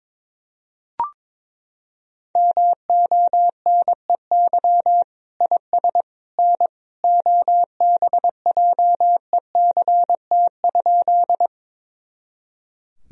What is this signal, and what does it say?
Morse code, heard as a keyed tone, about 22 wpm, 700 Hz. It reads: MONEY IS N OBJECT?
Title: Sentences from Top 500 Words and Abbreviations 22wpm